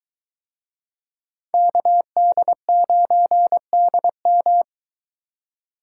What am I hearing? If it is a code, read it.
Morse code, heard as a keyed tone, about 22 wpm, 700 Hz. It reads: KD9DM